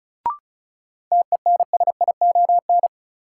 Morse code, heard as a keyed tone, about 36 wpm, 700 Hz. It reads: TENSION
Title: Sentences from Top 1600 Words - New Word Drill Alt - 35wpm